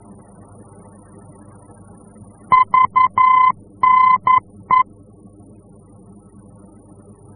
Morse code, heard as a keyed tone, about 11 wpm, 1020 Hz. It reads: VNE
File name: france / mx / VNE-F-VOR